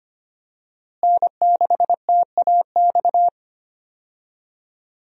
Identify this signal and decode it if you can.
Morse code, heard as a keyed tone, about 25 wpm, 700 Hz. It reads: N6TAX
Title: USA Call Signs - 5 Characters - Repeated 3x - 25wpm